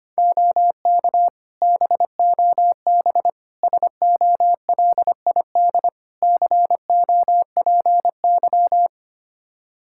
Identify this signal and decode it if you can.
Morse code, heard as a keyed tone, about 25 wpm, 700 Hz. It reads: OK BOB SOLID COPY
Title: Practice QSO - Exchange 3 - More Info - 25wpm